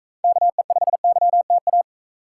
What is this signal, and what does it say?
Morse code, heard as a keyed tone, about 42 wpm, 700 Hz. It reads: KE5YTA